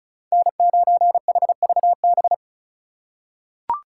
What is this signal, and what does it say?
Morse code, heard as a keyed tone, about 36 wpm, 700 Hz. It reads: N9HVB
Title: USA Call Signs - 3x Character Spacing - 35wpm